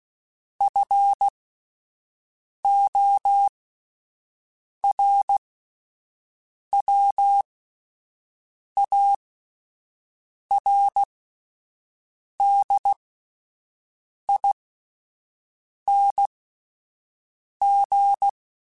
Morse code, heard as a keyed tone, about 16 wpm, spaced as for 5 wpm, 800 Hz. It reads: FORWARDING
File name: Spaced / Chuckles Spaced-mp3s / Temptation sp @10